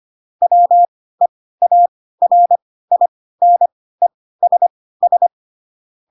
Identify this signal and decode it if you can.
Morse code, heard as a keyed tone, about 25 wpm, spaced as for 16 wpm, 700 Hz. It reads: WEARINESS